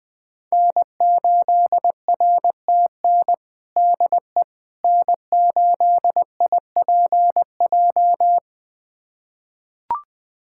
Morse code, heard as a keyed tone, about 20 wpm, 700 Hz. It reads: N8RTN DE N8IPJ
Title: USA Callsign DE Callsign - 5 Characters - 20wpm